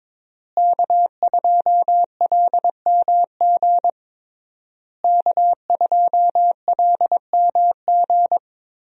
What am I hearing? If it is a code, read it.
Morse code, heard as a keyed tone, about 21 wpm, 700 Hz. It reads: K2LMG K2LMG
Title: USA Call Signs - 5 Characters - Repeated 3x - 22wpm